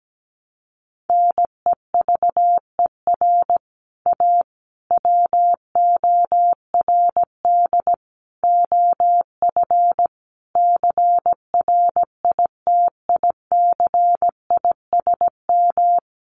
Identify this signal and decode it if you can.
Morse code, heard as a keyed tone, about 17 wpm, 700 Hz. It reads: NEVER A WORD OF CRITICISM
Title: Sentences from Top 1300 Words 17wpm